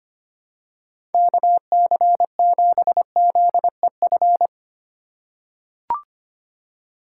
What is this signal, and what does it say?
Morse code, heard as a keyed tone, about 25 wpm, 700 Hz. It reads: KC7ZEF
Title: USA Call Signs - 6 Characters - 2x Character Spacing - 25wpm